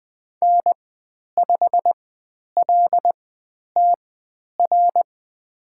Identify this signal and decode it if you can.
Morse code, heard as a keyed tone, about 20 wpm, spaced as for 10 wpm, 700 Hz. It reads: N5LTR
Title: 109 Lesson 18g - Letter m - Call Signs 20wpm